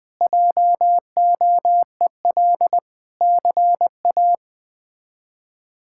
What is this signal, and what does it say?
Morse code, heard as a keyed tone, about 20 wpm, 700 Hz. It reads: JOEL CA
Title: K1USN SST QSO Practice - 20wpm